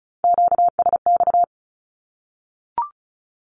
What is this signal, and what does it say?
Morse code, heard as a keyed tone, about 34 wpm, 700 Hz. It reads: QSX
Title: Q-Codes Repeated 3x 35wpm